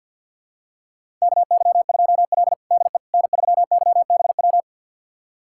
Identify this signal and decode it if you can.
Morse code, heard as a keyed tone, about 47 wpm, 700 Hz. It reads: KY2L DE N3YBW
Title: USA Callsign DE Callsign 50wpm